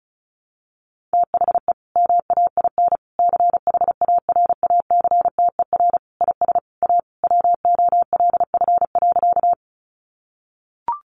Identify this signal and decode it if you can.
Morse code, heard as a keyed tone, about 34 wpm, 700 Hz. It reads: THE MAIN CHARACTER IS A WOLF.
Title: Additional Easy Phrases 35wpm